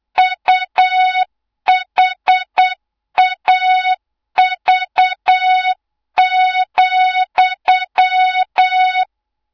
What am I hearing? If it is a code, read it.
Morse code, heard as a keyed tone, about 8 wpm, 3700 Hz, with noise. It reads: UHAV,
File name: Morze-HA2MN / hang / 24 GYAKORLAT